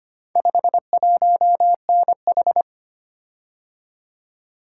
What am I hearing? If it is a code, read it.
Morse code, heard as a keyed tone, about 25 wpm, 700 Hz. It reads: 51NH
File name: practice / 25wpm / Four Letter-Number Combinations - 25wpm